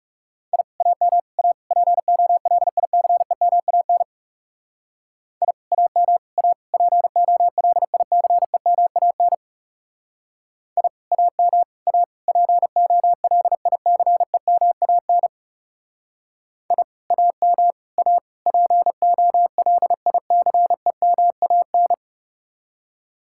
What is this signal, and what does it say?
Morse code, heard as a keyed tone, about 36 wpm, 700 Hz. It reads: I AM A POLICEMAN I AM A POLICEMAN I AM A POLICEMAN I AM A POLICEMAN
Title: Speed-Racing Common QSO Phrases - 30wpm